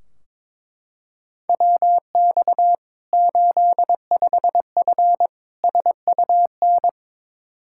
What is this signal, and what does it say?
Morse code, heard as a keyed tone, about 22 wpm, 700 Hz. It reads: WX 85F SUN
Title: Practice QSO - Exchange 2 - Info - 22wpm